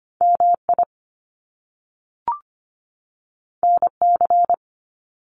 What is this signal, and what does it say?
Morse code, heard as a keyed tone, about 25 wpm, 700 Hz. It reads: MI NC
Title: US States 25wpm